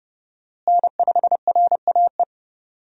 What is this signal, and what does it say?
Morse code, heard as a keyed tone, about 31 wpm, 700 Hz. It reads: N5RAE
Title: USA Call Signs - 5 Characters - Repeated 3x - 30wpm